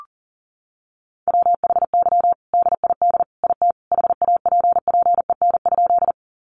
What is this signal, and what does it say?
Morse code, heard as a keyed tone, about 37 wpm, 700 Hz. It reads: WHY DID IT HAPPEN?
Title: Sentences from Top 400 Words 40wpm